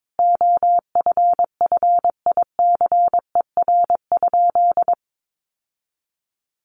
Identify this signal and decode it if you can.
Morse code, heard as a keyed tone, about 22 wpm, 700 Hz. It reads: OFFICER?
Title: Sentences from Top 1200 Words 22wpm